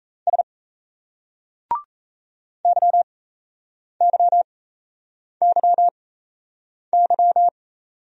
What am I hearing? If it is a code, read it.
Morse code, heard as a keyed tone, about 38 wpm, spaced as for 8 wpm, 700 Hz. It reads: S YYYY